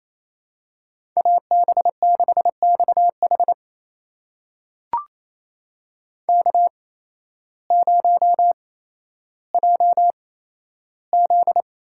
Morse code, heard as a keyed tone, about 28 wpm, spaced as for 13 wpm, 700 Hz. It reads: AB6XH K0JZ
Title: USA Call Signs - 8x Character Spacing - 28wpm